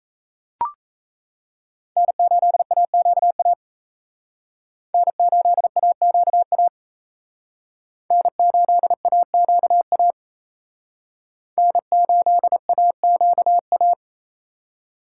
Morse code, heard as a keyed tone, about 35 wpm, 700 Hz. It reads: N8AQA N8AQA N8AQA N8AQA
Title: Speed Racing - USA 1x3 Call Signs 28wpm